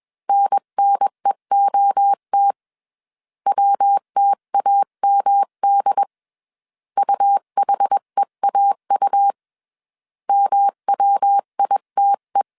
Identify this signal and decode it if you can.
Morse code, heard as a keyed tone, about 21 wpm, 800 Hz. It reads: NNEOT WTAMD UHEAU MWITE